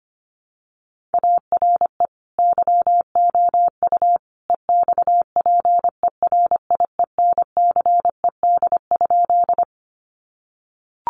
Morse code, heard as a keyed tone, about 25 wpm, 700 Hz. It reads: ARE YOU EXPERIENCED?
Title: Sentences from Top 1100 Words 25wpm